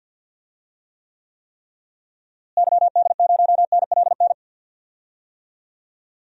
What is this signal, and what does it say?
Morse code, heard as a keyed tone, about 53 wpm, 700 Hz. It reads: YD0NLN
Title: DXCC-100 Call Signs - 6 Characters - Repeated 3x - 50wpm